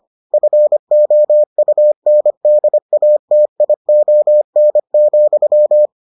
Morse code, heard as a keyed tone, about 25 wpm, 600 Hz. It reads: FOUNDATION,